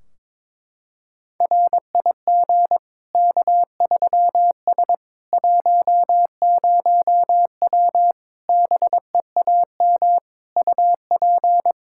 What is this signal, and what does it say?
Morse code, heard as a keyed tone, about 21 wpm, 700 Hz. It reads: RIG K3S 10W BEAM UP
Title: Practice QSO - Exchange 2 - Info - 22wpm